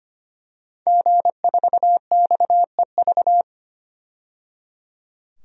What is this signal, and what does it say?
Morse code, heard as a keyed tone, about 25 wpm, 700 Hz. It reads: G4XEV
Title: DXCC-100 Call Signs - 5 Characters - 25wpm